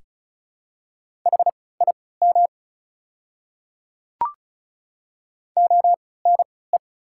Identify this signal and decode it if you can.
Morse code, heard as a keyed tone, about 36 wpm, spaced as for 20 wpm, 700 Hz. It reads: HIM ONE
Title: 3x Character Spacing - Top 300 Words - 35wpm